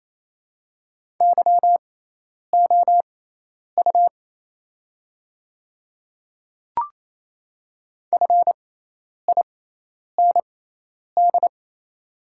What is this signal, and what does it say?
Morse code, heard as a keyed tone, about 28 wpm, spaced as for 10 wpm, 690 Hz. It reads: YOU FIND
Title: 6x Character Spacing - Top 200 Words - 28wpm